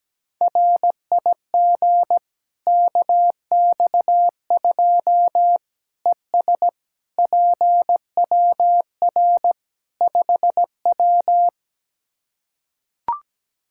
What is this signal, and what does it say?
Morse code, heard as a keyed tone, about 17 wpm, 700 Hz. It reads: RIG KX2 ES PWR 5W
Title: K1USN SST QSO Practice 17wpm